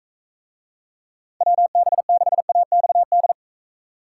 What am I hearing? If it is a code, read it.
Morse code, heard as a keyed tone, about 42 wpm, 700 Hz. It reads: WB6AXD